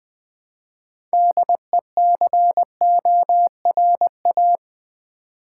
Morse code, heard as a keyed tone, about 20 wpm, 700 Hz. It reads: DECORA
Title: Build A Word - 3 Letters At A Time - 8-Letter Words - 20wpm